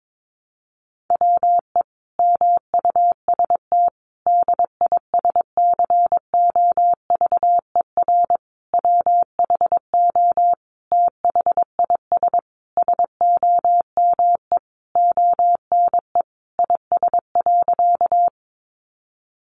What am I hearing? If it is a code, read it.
Morse code, heard as a keyed tone, about 22 wpm, 700 Hz. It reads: WE MUST DISCOVER WHO THIS SOME ONE IS.
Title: Sentences from Top 1100 Words 22wpm